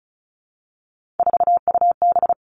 Morse code, heard as a keyed tone, about 36 wpm, 700 Hz. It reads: 4UB
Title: Three Challenging Letter-Number Combinations 35wpm